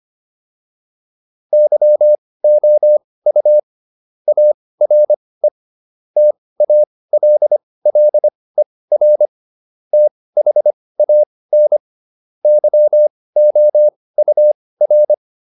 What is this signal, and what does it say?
Morse code, heard as a keyed tone, about 25 wpm, spaced as for 18 wpm, 600 Hz. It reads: YOU ARE TALLER THAN YOUR
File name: SS123 18